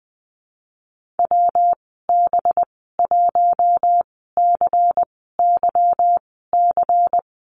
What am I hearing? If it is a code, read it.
Morse code, heard as a keyed tone, about 20 wpm, spaced as for 15 wpm, 700 Hz. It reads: WB1CYC
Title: 238 Lesson 38g - Letter q - Call Signs 20wpm